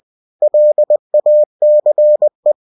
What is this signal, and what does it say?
Morse code, heard as a keyed tone, about 20 wpm, 600 Hz. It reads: LACE
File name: cw-books / winnie-the-pooh / winnie-20wpm-page-0028